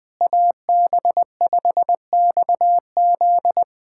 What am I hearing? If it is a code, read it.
Morse code, heard as a keyed tone, about 20 wpm, 700 Hz. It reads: AB5XZ
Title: USA Callsign DE Callsign 20wpm